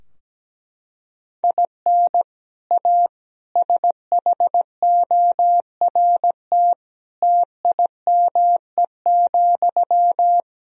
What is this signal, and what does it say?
Morse code, heard as a keyed tone, about 17 wpm, 700 Hz. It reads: IN A SHORT TIME,
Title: Sentences from Top 1000 Words - Review Part 2 - 17wpm